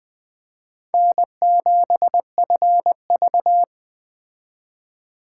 Morse code, heard as a keyed tone, about 21 wpm, 700 Hz. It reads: N7FV